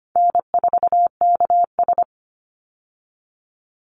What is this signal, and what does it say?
Morse code, heard as a keyed tone, about 25 wpm, 700 Hz. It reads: N4KS
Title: Easy Call Signs - 25wpm